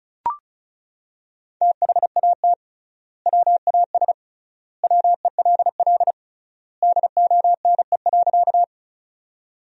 Morse code, heard as a keyed tone, about 34 wpm, 700 Hz. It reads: THAT WAS WELL DONE.
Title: 3x Word-Spacing - Sentences from Top 500 Words - 35wpm